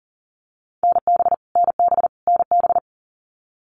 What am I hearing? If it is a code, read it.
Morse code, heard as a keyed tone, about 40 wpm, 700 Hz. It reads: NB NB NB
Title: Canadian Provinces and Territories - Repeated 3x - 40wpm